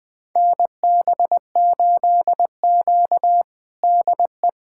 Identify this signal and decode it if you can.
Morse code, heard as a keyed tone, about 21 wpm, 700 Hz. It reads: NB8Q DE
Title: USA Callsign DE Callsign - 4 Characters - 20wpm